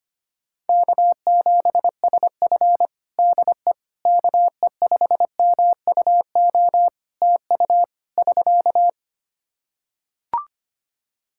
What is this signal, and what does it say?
Morse code, heard as a keyed tone, about 25 wpm, 700 Hz. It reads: K7SF DE KE5MUO TU <SK>
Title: K1USN SST QSO Practice 25wpm